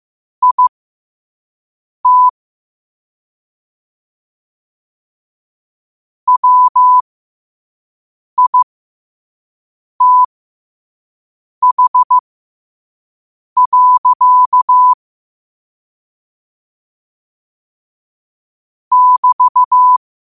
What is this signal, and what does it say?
Morse code, heard as a keyed tone, about 15 wpm, spaced as for 5 wpm, 1000 Hz. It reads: IT WITH. <BT>
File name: qotdfiles / 5wpm